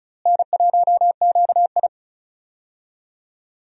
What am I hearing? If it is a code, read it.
Morse code, heard as a keyed tone, about 36 wpm, 700 Hz. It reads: N1QI
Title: USA Call Signs - 4 Characters - Repeated 3x - 35wpm